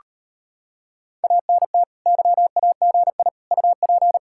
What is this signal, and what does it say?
Morse code, heard as a keyed tone, about 38 wpm, 700 Hz. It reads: ANT YAGI UP